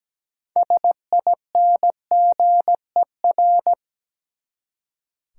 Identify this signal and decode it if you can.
Morse code, heard as a keyed tone, about 17 wpm, 700 Hz. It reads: SINGER
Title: Six Letter Words - 17wpm